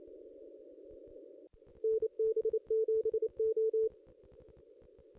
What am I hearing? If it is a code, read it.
Morse code, heard as a keyed tone, about 28 wpm, 440 Hz. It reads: NB7O
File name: contests / ss-cw-k5tr-op-n5rz -2200